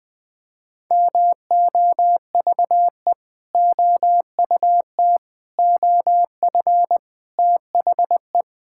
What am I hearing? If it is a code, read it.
Morse code, heard as a keyed tone, about 20 wpm, 700 Hz. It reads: MOVE OUT OF THE
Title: Sentences with 4-Word Phrases - Spoken Beginning - 20wpm